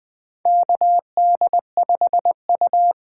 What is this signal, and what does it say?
Morse code, heard as a keyed tone, about 20 wpm, 700 Hz. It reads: KD5U
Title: USA Call Signs - 6 Characters - 8x Character Spacing - 20wpm